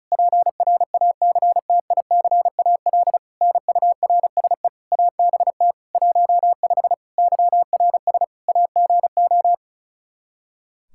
Morse code, heard as a keyed tone, about 36 wpm, 700 Hz. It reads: PRACTICAL NURSE ABT 15 YRS AGO
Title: K1USN SST QSO Practice 35wpm